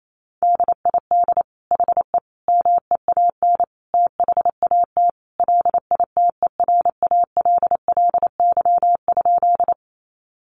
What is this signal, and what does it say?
Morse code, heard as a keyed tone, about 28 wpm, 700 Hz. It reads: DID HE MEAN THAT LITERALLY?
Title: Sentences from Top 1400 Words 28wpm